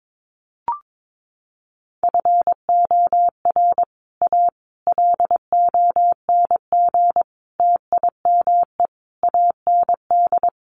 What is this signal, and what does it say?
Morse code, heard as a keyed tone, about 22 wpm, 700 Hz. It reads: FOR A LONG TIME AND
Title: Sets of 5 Words Encore2 22wpm